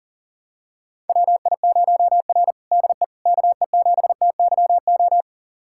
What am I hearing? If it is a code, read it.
Morse code, heard as a keyed tone, about 37 wpm, 700 Hz. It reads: WI0R DE KE7TYO
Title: USA Callsign DE Callsign 40wpm